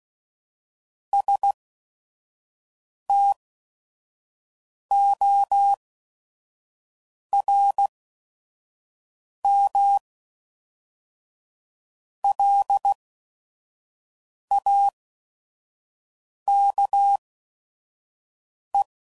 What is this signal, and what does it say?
Morse code, heard as a keyed tone, about 16 wpm, spaced as for 5 wpm, 800 Hz. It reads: STORMLAKE
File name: CW Code Groups @5WPM / fromSM's / Cities